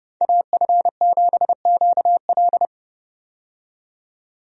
Morse code, heard as a keyed tone, about 32 wpm, 700 Hz. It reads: AF7QL
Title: USA Call Signs - 5 Characters - 30wpm